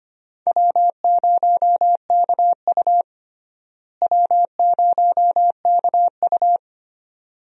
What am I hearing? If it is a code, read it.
Morse code, heard as a keyed tone, about 25 wpm, 700 Hz. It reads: W0KU W0KU
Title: USA Call Signs - 4 Characters - Repeated 2x - 25wpm